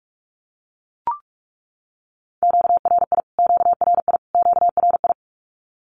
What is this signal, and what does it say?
Morse code, heard as a keyed tone, about 44 wpm, 700 Hz. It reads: QRI QRI QRI
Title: Q-Codes Repeated 3x 45wpm